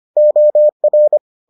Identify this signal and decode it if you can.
Morse code, heard as a keyed tone, about 25 wpm, 600 Hz. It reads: OR